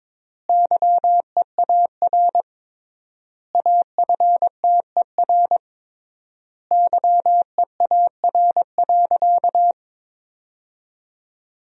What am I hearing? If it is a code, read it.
Morse code, heard as a keyed tone, about 22 wpm, 700 Hz. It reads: YEAR AFTER YEAR.
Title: 3x Word-Spacing - Sentences from Top 300 Words - 22wpm